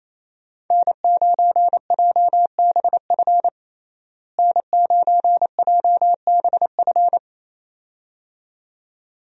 Morse code, heard as a keyed tone, about 28 wpm, 700 Hz. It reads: N9JBF N9JBF
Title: USA Call Signs - 5 Characters - Repeated 2x - 28wpm